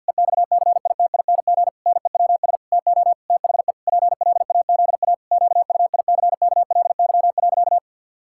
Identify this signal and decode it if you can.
Morse code, heard as a keyed tone, about 49 wpm, 700 Hz. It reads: EXCITING NEWS TO THE PLAZA QUICKLY.
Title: Pangram Challenge v2 - 50wpm